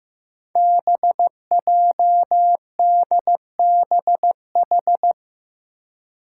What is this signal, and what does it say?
Morse code, heard as a keyed tone, about 15 wpm, 700 Hz. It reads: BJDBH
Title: Five Challenging Letter-Number Combinations 15wpm